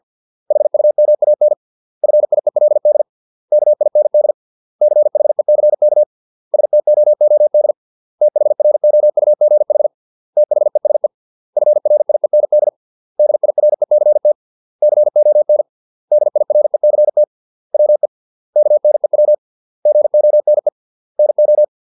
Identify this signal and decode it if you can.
Morse code, heard as a keyed tone, about 50 wpm, 600 Hz. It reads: HUMAN FIELD KIND CHECK STOOD THROUGH THESE FRIEND DIRECT CON DIRECT WE KNEW GONE NO
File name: 632 words / 632 Words 50WPM